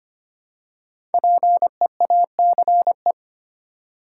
Morse code, heard as a keyed tone, about 25 wpm, 700 Hz. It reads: PEACE
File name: 5 Letter Words - First 3 Letters Given - 25wpm